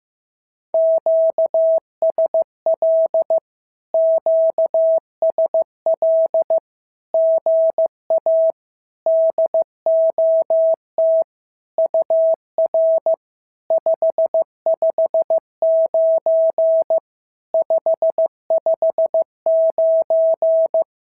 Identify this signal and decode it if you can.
Morse code, heard as a keyed tone, about 15 wpm, 650 Hz. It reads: QSL QSL GA DOT UR 559 559
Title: SOTA QSO Practice 15wpm